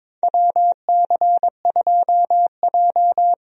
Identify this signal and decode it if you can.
Morse code, heard as a keyed tone, about 21 wpm, 700 Hz. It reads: WC2J
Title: USA Call Signs - 4 Characters - Repeated 3x - 22wpm